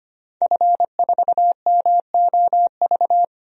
Morse code, heard as a keyed tone, about 25 wpm, 700 Hz. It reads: F4MOV
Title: DXCC-100 Call Signs - 5 Characters - 25wpm